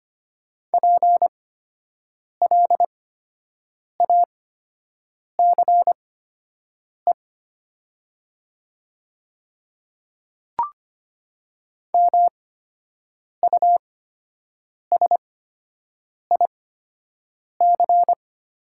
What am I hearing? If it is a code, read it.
Morse code, heard as a keyed tone, about 25 wpm, spaced as for 7 wpm, 700 Hz. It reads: PLACE MUSIC
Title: 8x Character Spacing - Top 300 Words - 25wpm